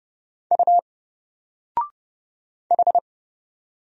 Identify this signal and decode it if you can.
Morse code, heard as a keyed tone, about 30 wpm, 700 Hz. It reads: U H
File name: Single-Letter-Number-30wpm